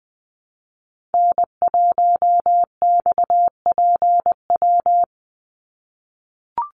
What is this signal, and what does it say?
Morse code, heard as a keyed tone, about 20 wpm, 700 Hz. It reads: N1XPW
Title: USA Call Signs 20wpm